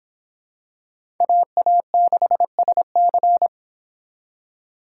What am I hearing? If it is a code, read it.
Morse code, heard as a keyed tone, about 25 wpm, 700 Hz. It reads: AA6SC